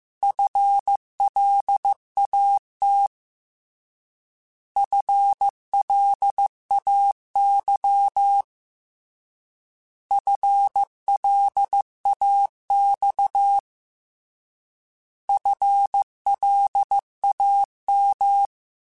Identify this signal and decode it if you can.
Morse code, heard as a keyed tone, about 15 wpm, 800 Hz. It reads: FLAT FLAY FLAX FLAM
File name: Alphabetical / Alphabetical mp3s / Fs @13